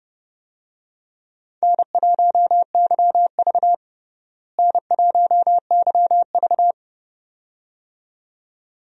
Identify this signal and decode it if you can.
Morse code, heard as a keyed tone, about 30 wpm, 700 Hz. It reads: N1YV N1YV
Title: USA Call Signs - 4 Characters - Repeated 2x - 30wpm